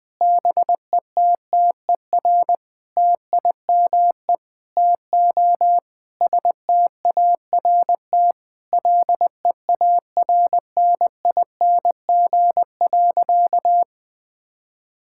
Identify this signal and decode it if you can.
Morse code, heard as a keyed tone, about 20 wpm, 700 Hz. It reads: BETTER TIME TO START LEARNING.
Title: Sentences with 7-Word Phrases - 20wpm